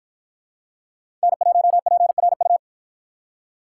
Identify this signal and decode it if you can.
Morse code, heard as a keyed tone, about 50 wpm, 700 Hz. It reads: N1PRU